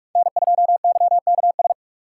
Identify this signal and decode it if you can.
Morse code, heard as a keyed tone, about 47 wpm, 700 Hz. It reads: N2YKS
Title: USA Call Signs - 5 Characters - 3x Character Spacing - 45wpm